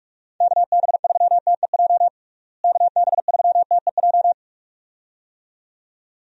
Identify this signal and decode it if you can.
Morse code, heard as a keyed tone, about 42 wpm, 700 Hz. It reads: KB3TEJ KB3TEJ
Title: USA Call Signs - 6 Characters - Repeated 2x - 45wpm